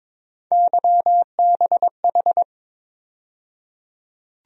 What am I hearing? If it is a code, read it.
Morse code, heard as a keyed tone, about 22 wpm, 700 Hz. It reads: YBH
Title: USA Call Sign Postfixes - 3 Letters 22wpm